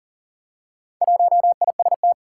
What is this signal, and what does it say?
Morse code, heard as a keyed tone, about 40 wpm, 700 Hz. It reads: 1IST
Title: Four Challenging Letter-Number Combinations 40wpm